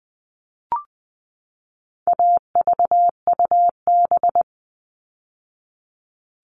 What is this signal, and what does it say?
Morse code, heard as a keyed tone, about 20 wpm, 700 Hz. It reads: AVUB
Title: All Cut Numbers - 4-Digit 20wpm